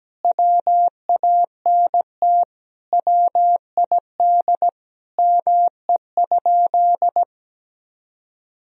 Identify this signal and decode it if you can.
Morse code, heard as a keyed tone, about 17 wpm, 700 Hz. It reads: WANT WID ME?
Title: Sentences from Top 200 Words and Abbreviations 17wpm